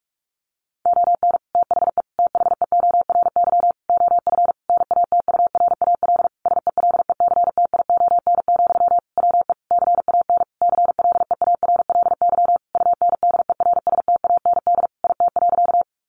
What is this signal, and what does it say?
Morse code, heard as a keyed tone, about 44 wpm, 700 Hz. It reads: ON THE THEORY OF NATURAL SELECTION, WE CAN CLEARLY UNDERSTAND IT.
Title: Sentences from Top 1400 Words 45wpm